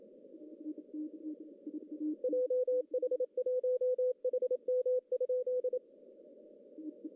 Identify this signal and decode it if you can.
Morse code, heard as a keyed tone, about 27 wpm, 500 Hz. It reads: JH1HM?
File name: CW / cqww-cw-k5tr